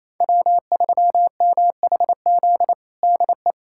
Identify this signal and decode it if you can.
Morse code, heard as a keyed tone, about 28 wpm, 700 Hz. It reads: W3MHZ DE
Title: USA Callsign DE Callsign - 5 Characters - 28wpm